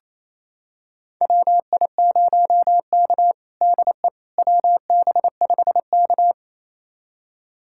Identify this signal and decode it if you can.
Morse code, heard as a keyed tone, about 28 wpm, 700 Hz. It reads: WI0K DE WB5K
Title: USA Callsign DE Callsign - 4 Characters - 28wpm